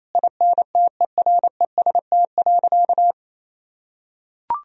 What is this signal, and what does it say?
Morse code, heard as a keyed tone, about 27 wpm, 700 Hz. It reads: INTEREST.
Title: Sentences from Top 1000 Words - Review Part 4 - 28wpm